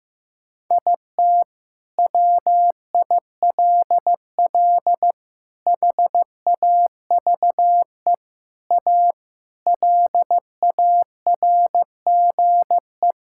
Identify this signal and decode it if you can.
Morse code, heard as a keyed tone, about 15 wpm, 700 Hz. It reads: IT WILL HAVE A LARGE
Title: Sentences from Top 1000 Words - Review Part 6 - 15wpm